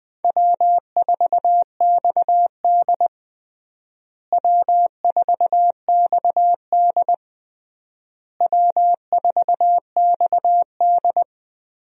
Morse code, heard as a keyed tone, about 20 wpm, 700 Hz. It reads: W4XD W4XD W4XD
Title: USA Call Signs - 4 Characters - Repeated 3x - 20wpm